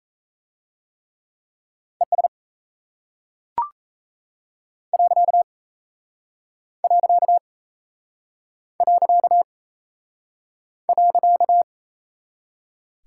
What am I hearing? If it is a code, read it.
Morse code, heard as a keyed tone, about 36 wpm, 700 Hz. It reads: ES . .